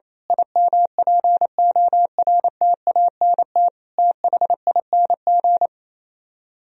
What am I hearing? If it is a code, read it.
Morse code, heard as a keyed tone, about 27 wpm, 700 Hz. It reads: IMPORTANT THING
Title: Speed-Racing - Sets of 4 Words 28wpm